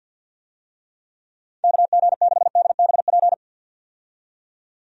Morse code, heard as a keyed tone, about 51 wpm, 700 Hz. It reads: KG6DBP
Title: USA Call Signs - 6 Characters - 50wpm